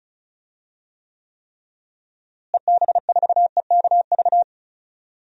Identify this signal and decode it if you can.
Morse code, heard as a keyed tone, about 34 wpm, 700 Hz. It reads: EB4EKV
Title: DXCC-100 Call Signs - 6 Characters - Repeated 3x - 35wpm